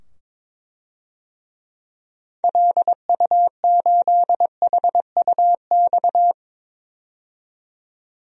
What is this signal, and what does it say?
Morse code, heard as a keyed tone, about 21 wpm, 700 Hz. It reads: LU8HUX
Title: DXCC-100 Call Signs - 6 Characters - 22wpm